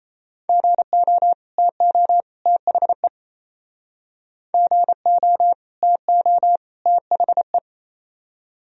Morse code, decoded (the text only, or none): GO TO THE GO TO THE